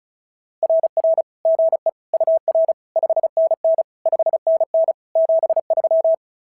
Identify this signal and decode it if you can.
Morse code, heard as a keyed tone, about 36 wpm, 650 Hz. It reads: RR GE UR 5NN 5NN 73
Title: SOTA QSO Practice 35wpm